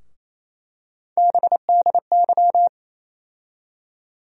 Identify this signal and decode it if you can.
Morse code, heard as a keyed tone, about 28 wpm, 700 Hz. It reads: BDY
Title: USA Call Sign Postfixes - 3 Letters 28wpm